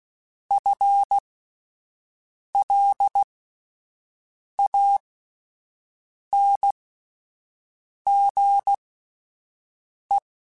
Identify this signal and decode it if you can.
Morse code, heard as a keyed tone, about 16 wpm, spaced as for 5 wpm, 800 Hz. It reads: FLANGE
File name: Spaced / FL Runs sp mp3s / FL Runs sp @10